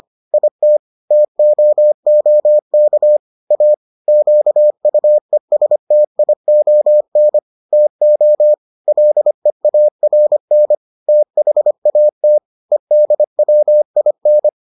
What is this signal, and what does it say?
Morse code, heard as a keyed tone, about 25 wpm, 600 Hz. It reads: IT TOOK A QUESTION TO LEARN THAT EDWIN